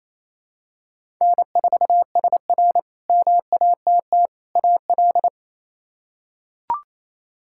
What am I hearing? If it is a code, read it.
Morse code, heard as a keyed tone, about 27 wpm, 700 Hz. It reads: N4SR MATT AL
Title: K1USN SST QSO Practice - 28wpm